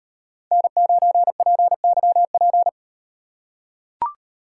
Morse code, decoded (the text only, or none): N9PYP